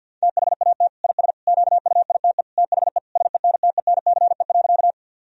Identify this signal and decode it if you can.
Morse code, heard as a keyed tone, about 52 wpm, 700 Hz. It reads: THAT IS QUITE THE SENTENCE.
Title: Sentences from Top 1000 Words - Review Part 1 - 50wpm